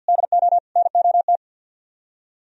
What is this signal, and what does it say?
Morse code, heard as a keyed tone, about 59 wpm, 700 Hz. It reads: DO NOT
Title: Speed-Racing - Sets of 4 Words 50wpm